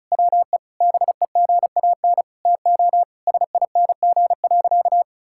Morse code, heard as a keyed tone, about 36 wpm, 700 Hz. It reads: WE BEGAN TO SING.
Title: Sentences from Top 1600 Words 35wpm